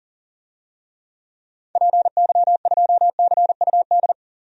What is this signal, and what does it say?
Morse code, heard as a keyed tone, about 40 wpm, 700 Hz. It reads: PY2CUD
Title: DXCC-100 Call Signs - 6 Characters - 40wpm